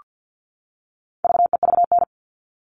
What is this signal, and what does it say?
Morse code, heard as a keyed tone, about 48 wpm, 700 Hz. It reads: VEVN
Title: All Cut Numbers - 4-Digit 50wpm